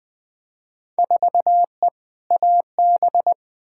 Morse code, heard as a keyed tone, about 21 wpm, 700 Hz. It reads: 4E AB
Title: Field Day Exchanges Encore - 20wpm